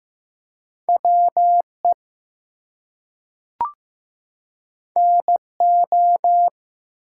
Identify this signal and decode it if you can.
Morse code, heard as a keyed tone, about 15 wpm, 700 Hz. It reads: WE NO
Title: Two Letter Words - 15wpm